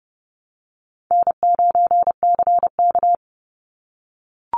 Morse code, heard as a keyed tone, about 30 wpm, 700 Hz. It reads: N9CK
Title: Easy Call Signs - 30wpm